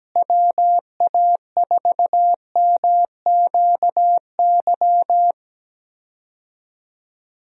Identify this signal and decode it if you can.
Morse code, heard as a keyed tone, about 17 wpm, 700 Hz. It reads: WA4MQY
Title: USA Call Signs - 6 Characters - Repeated 3x - 17wpm